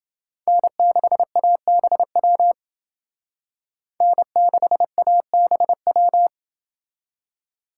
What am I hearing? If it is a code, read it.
Morse code, heard as a keyed tone, about 29 wpm, 700 Hz. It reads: N6ABW N6ABW